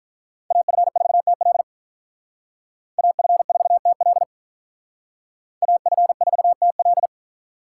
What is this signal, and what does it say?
Morse code, heard as a keyed tone, about 44 wpm, 700 Hz. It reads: AF4TL AF4TL AF4TL
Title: Speed Racing - USA 2x2 Call Signs 35wpm